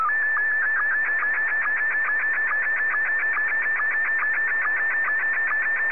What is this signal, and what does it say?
SSTV (slow-scan television), Scottie 1.